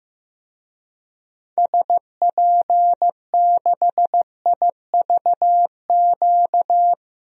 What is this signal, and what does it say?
Morse code, heard as a keyed tone, about 15 wpm, 700 Hz. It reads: SP6IVQ
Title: DXCC-100 Call Signs - 6 Characters - Repeated 2x - 15wpm